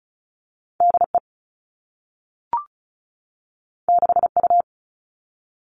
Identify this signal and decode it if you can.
Morse code, heard as a keyed tone, about 36 wpm, 700 Hz. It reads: DE 6U
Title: All Cut Numbers - 2-Digit 35wpm